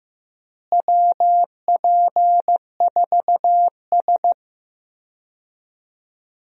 Morse code, heard as a keyed tone, about 15 wpm, 700 Hz. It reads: WP4S
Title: USA Call Signs - 4 Characters - Repeated 3x - 15wpm